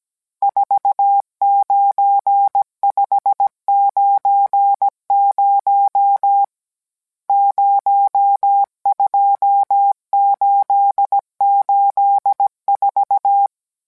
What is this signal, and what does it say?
Morse code, heard as a keyed tone, about 17 wpm, 800 Hz. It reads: 49590 02884